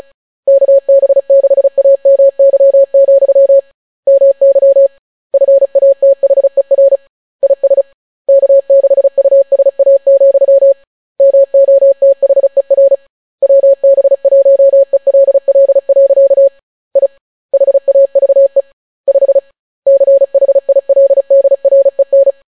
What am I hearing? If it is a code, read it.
Morse code, heard as a keyed tone, about 36 wpm, 550 Hz. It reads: KB6AMY, MY FATHER IS K6USA, MOTHER WB1ELL. I HAVE 5 CHILDREN